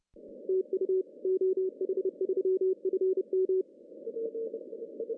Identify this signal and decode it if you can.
Morse code, heard as a keyed tone, about 30 wpm, 380 Hz. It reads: TU OH3FM
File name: CW / cqww-cw-k5tr